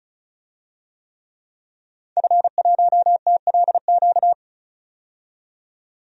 Morse code, heard as a keyed tone, about 36 wpm, 700 Hz. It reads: F1TLQ
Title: DXCC-100 Call Signs - 5 Characters - 35wpm